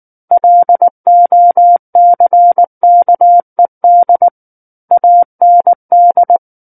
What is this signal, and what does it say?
Morse code, heard as a keyed tone, about 19 wpm, 700 Hz. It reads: LOCKED AND
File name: Book One / 10-Chapter 10 - 19 WPM